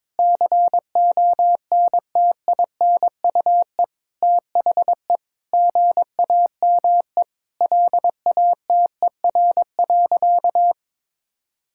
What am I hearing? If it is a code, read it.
Morse code, heard as a keyed tone, about 22 wpm, 700 Hz. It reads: CONTINUE THE GAME LATER.
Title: Sentences from Top 1000 Words - Review Part 6 - 22wpm